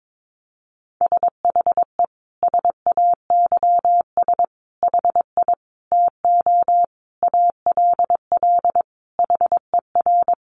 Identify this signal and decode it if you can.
Morse code, heard as a keyed tone, about 22 wpm, 700 Hz. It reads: SHE SAYS HI TO ALL HER
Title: Easy Phrases 22wpm